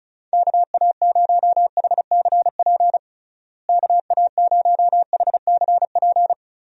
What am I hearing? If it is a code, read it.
Morse code, heard as a keyed tone, about 34 wpm, 700 Hz. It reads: KA0HCP KA0HCP
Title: USA Call Signs - 6 Characters - Repeated 3x - 35wpm